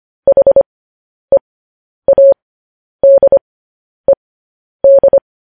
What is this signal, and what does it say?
Morse code, heard as a keyed tone, about 25 wpm, spaced as for 10 wpm, 550 Hz. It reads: HEADED
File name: podcast / News headlines-1x-25x10